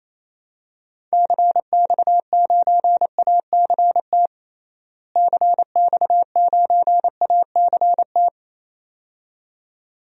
Morse code, heard as a keyed tone, about 28 wpm, 700 Hz. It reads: CX9ACT CX9ACT
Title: DXCC-100 Call Signs - 6 Characters - Repeated 2x - 28wpm